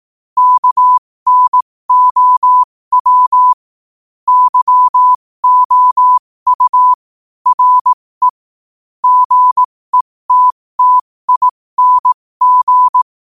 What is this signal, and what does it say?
Morse code, heard as a keyed tone, about 18 wpm, spaced as for 15 wpm, 1000 Hz. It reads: KNOW YOU RE GETTING
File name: qotdfiles / 15wpm